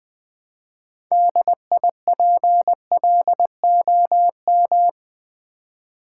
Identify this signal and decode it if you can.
Morse code, heard as a keyed tone, about 20 wpm, 700 Hz. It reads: DIPLOM
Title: Build A Word - 2 Letters At A Time - 8-Letter Words - 20wpm